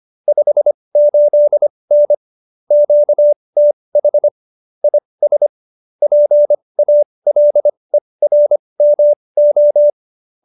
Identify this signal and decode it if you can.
Morse code, heard as a keyed tone, about 25 wpm, spaced as for 20 wpm, 590 Hz. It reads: 58N QTH IS PALERMO
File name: qso215b 20